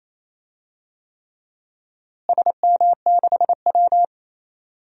Morse code, heard as a keyed tone, about 28 wpm, 700 Hz. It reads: SM6W
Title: DXCC-100 Call Signs - 4 Characters - Repeated 3x - 28wpm